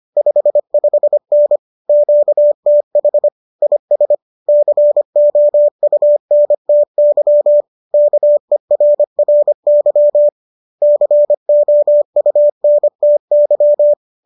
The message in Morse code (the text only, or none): 55N QTH IS COUNTY KERRY COUNTY